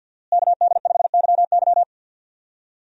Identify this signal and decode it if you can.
Morse code, heard as a keyed tone, about 47 wpm, 700 Hz. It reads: KD5YY
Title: USA Call Signs - 5 Characters - Repeated 2x - 50wpm